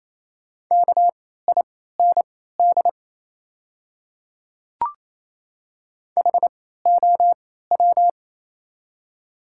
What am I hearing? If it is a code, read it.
Morse code, heard as a keyed tone, about 27 wpm, spaced as for 16 wpm, 700 Hz. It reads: KIND HOW
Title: 3x Character Spacing - Top 200 Words - 28wpm